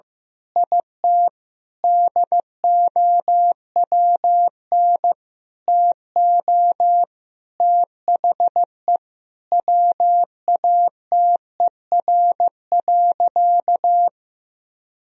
Morse code, heard as a keyed tone, about 15 wpm, 700 Hz. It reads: IT DOWN TO THE WATER.
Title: Sentences from Top 1000 Words - Review Part 4 - 15wpm